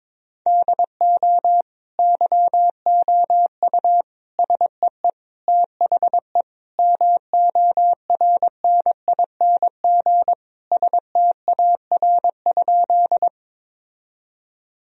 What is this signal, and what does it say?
Morse code, heard as a keyed tone, about 22 wpm, 700 Hz. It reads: DO YOU SEE THE MORNING STAR?
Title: Sentences from Top 1000 Words - Review Part 2 - 22wpm